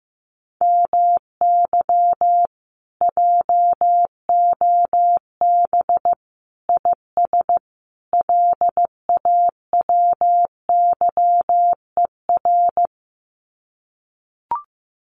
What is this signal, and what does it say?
Morse code, heard as a keyed tone, about 15 wpm, 700 Hz. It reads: MY JOB IS LAWYER
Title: Common QSO Phrases Encore 15wpm